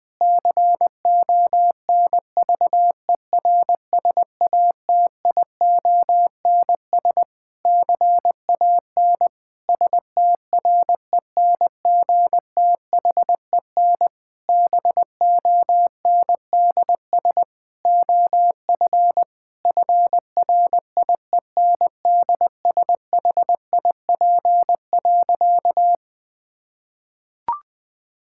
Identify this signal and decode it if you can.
Morse code, heard as a keyed tone, about 21 wpm, 700 Hz. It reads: CONVERSATIONS CAN STRENGTHEN BONDS OF FRIENDSHIP.
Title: Difficult Sentences with Hints - 20wpm